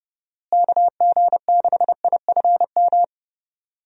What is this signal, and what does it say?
Morse code, heard as a keyed tone, about 30 wpm, 700 Hz. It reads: KG6IFM
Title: USA Call Signs - 6 Characters - Repeated 2x - 30wpm